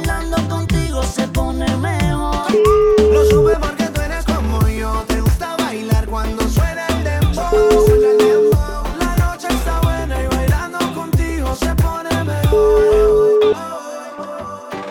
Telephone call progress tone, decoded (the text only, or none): ringback tone